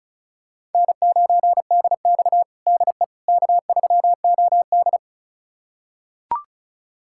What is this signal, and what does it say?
Morse code, heard as a keyed tone, about 36 wpm, 700 Hz. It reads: N9DX DE K3OD
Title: USA Callsign DE Callsign - 4 Characters - 35wpm